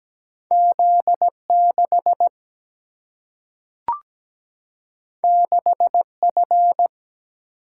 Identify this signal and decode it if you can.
Morse code, heard as a keyed tone, about 17 wpm, 700 Hz. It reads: Z 6 6 F
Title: Single-Letter Single-Number 17wpm